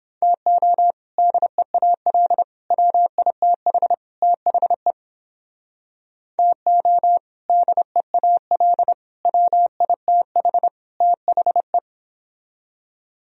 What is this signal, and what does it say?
Morse code, heard as a keyed tone, about 30 wpm, 700 Hz. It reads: TO DEAL WITH THE TO DEAL WITH THE